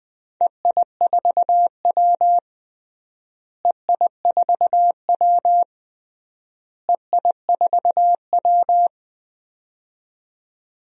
Morse code, heard as a keyed tone, about 21 wpm, 700 Hz. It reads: EI4W EI4W EI4W
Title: DXCC-100 Call Signs - 4 Characters - Repeated 3x - 20wpm